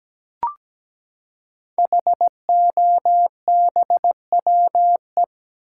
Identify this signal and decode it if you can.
Morse code, heard as a keyed tone, about 17 wpm, 700 Hz. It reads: HOBWE
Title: Code Groups - 5 Letters - 17wpm